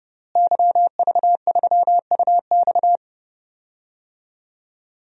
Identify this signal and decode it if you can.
Morse code, heard as a keyed tone, about 31 wpm, 700 Hz. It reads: YV3UX
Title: DXCC-100 Call Signs - 5 Characters - Repeated 3x - 30wpm